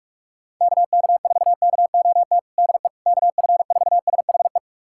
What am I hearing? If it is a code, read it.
Morse code, heard as a keyed tone, about 42 wpm, 700 Hz. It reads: KK4KOT DE KF4SHE
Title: USA Callsign DE Callsign 45wpm